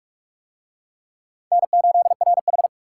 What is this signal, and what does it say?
Morse code, heard as a keyed tone, about 43 wpm, 700 Hz. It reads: N8RH